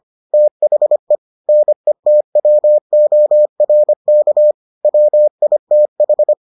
Morse code, heard as a keyed tone, about 25 wpm, 600 Hz. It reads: THE NETWORK WITH